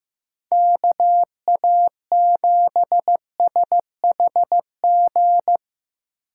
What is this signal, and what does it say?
Morse code, heard as a keyed tone, about 18 wpm, 700 Hz. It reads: KA7SHG
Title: Speed Racing - USA 2x3 Call Signs 15wpm